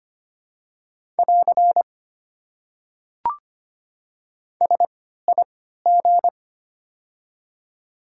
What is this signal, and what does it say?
Morse code, heard as a keyed tone, about 25 wpm, spaced as for 14 wpm, 700 Hz. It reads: <AR> SIG
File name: Top 90 QSO Elements - 3x Character Spacing - 25wpm